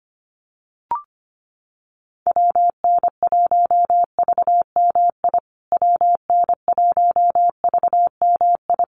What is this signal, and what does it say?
Morse code, heard as a keyed tone, about 25 wpm, 700 Hz. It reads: WN1VMI WN1VMI
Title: USA Call Signs Repeated 3x 25wpm